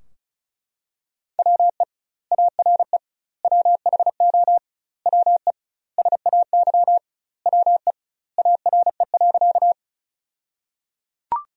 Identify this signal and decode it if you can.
Morse code, heard as a keyed tone, about 36 wpm, 700 Hz. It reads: WE ARE WHO WE SAY WE ARE.
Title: 2x Word-Spacing - Sentences from Top 200 Words - 35wpm